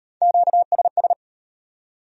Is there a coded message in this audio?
Yes: Morse code from 0.2 to 1.1 s, heard as a keyed tone, about 38 wpm, 700 Hz. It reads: QSS